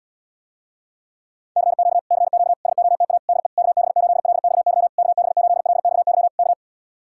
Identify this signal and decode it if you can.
Morse code, heard as a keyed tone, about 77 wpm, 700 Hz. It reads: CQ CQ SOTA DE KB8LXY KB8LXY K